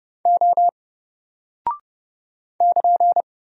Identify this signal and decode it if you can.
Morse code, heard as a keyed tone, about 29 wpm, 700 Hz. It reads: O <KN>
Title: Single Character (Letter:Number:Punctuation:Prosign) Alt 30wpm